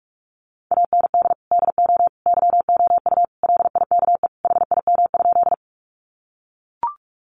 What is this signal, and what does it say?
Morse code, heard as a keyed tone, about 44 wpm, 700 Hz. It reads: AND DO YOU LIKE HIM?
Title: Sentences from Top 100 Words 45wpm